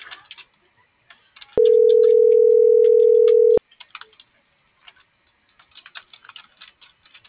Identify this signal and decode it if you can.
Ringback tone (North America), 440 + 480 Hz.